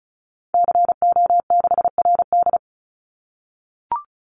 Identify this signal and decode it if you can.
Morse code, heard as a keyed tone, about 34 wpm, 700 Hz. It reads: CO6RD
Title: Call Signs - 35wpm